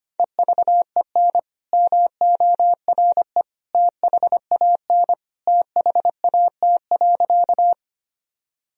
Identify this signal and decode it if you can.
Morse code, heard as a keyed tone, about 25 wpm, 700 Hz. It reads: EVEN MORE THAN THAT.
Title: Sentences from Top 1000 Words - Review Part 1 - 25wpm